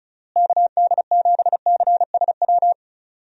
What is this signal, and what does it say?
Morse code, heard as a keyed tone, about 34 wpm, 700 Hz. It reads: KD7CSW
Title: USA Call Signs - 6 Characters - Repeated 3x - 35wpm